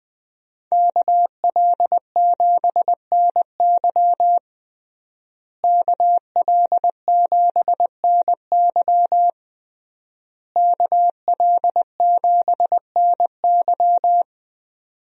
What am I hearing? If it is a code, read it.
Morse code, heard as a keyed tone, about 21 wpm, 700 Hz. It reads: KL7NY KL7NY KL7NY
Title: USA Call Signs - 5 Characters - Repeated 3x - 20wpm